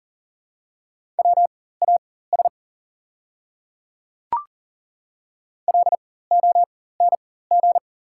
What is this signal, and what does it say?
Morse code, heard as a keyed tone, about 40 wpm, spaced as for 19 wpm, 700 Hz. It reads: WAS LONG
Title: 4x Character Spacing - Top 100 Words - 40wpm